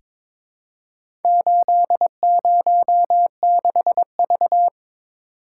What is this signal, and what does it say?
Morse code, heard as a keyed tone, about 22 wpm, 700 Hz. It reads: 806V
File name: practice / 22wpm / Four Letter-Number Combinations - 22wpm